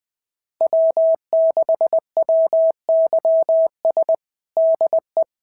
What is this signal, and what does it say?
Morse code, heard as a keyed tone, about 19 wpm, 650 Hz. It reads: W6WYS DE
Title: SOTA QSO Practice 20wpm